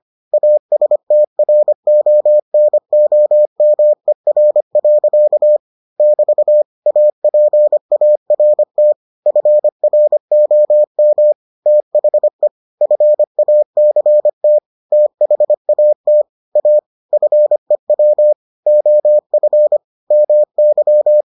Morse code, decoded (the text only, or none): ASTRONOMER. <BT> APART FROM THE FACT THAT A FEW OF MY